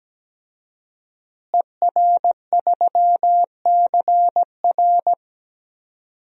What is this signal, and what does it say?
Morse code, heard as a keyed tone, about 17 wpm, 700 Hz. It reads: ER3CR
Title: DXCC-100 Call Signs - 5 Characters - Repeated 2x - 17wpm